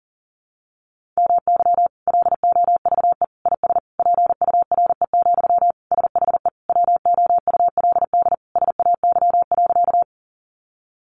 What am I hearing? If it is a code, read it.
Morse code, heard as a keyed tone, about 40 wpm, 700 Hz. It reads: MY LOVE IS PURE, SHE WOULD SAY.
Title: Sentences from Top 1500 Words 40wpm